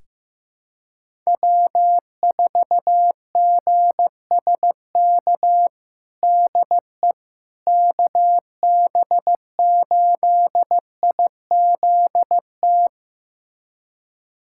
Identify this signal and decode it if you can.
Morse code, heard as a keyed tone, about 16 wpm, 700 Hz. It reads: W4GSK DE KB8IZT
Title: USA Callsign DE Callsign 15wpm